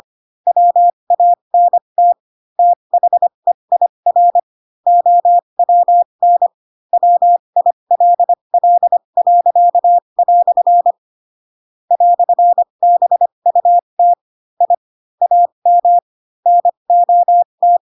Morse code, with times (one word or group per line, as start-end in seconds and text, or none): 0.5-2.1 WANT
2.6-4.4 THEIR
4.9-6.5 OWN
6.9-10.9 WILL."
11.9-14.1 "BUT
14.6-14.7 I
15.2-16.0 AM
16.5-17.8 NOT